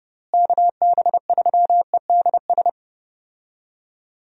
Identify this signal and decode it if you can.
Morse code, heard as a keyed tone, about 29 wpm, 700 Hz. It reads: KB3EDS
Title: USA Call Signs - 6 Characters - Repeated 3x - 30wpm